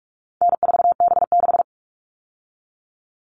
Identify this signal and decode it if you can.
Morse code, heard as a keyed tone, about 44 wpm, 700 Hz. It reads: N4B6
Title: All Cut Numbers - 4-Digit 45wpm